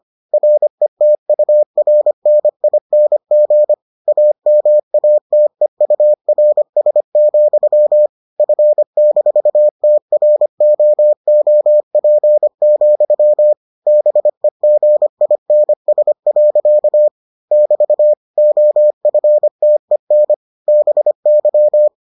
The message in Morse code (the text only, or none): RETURNING AMATEURS, F-TROOP, BEGINS. <BT> OFTEN BY